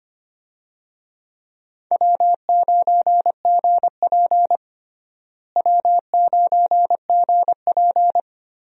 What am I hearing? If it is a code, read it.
Morse code, heard as a keyed tone, about 25 wpm, 700 Hz. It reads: W9GP W9GP
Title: USA Call Signs - 4 Characters - Repeated 2x - 25wpm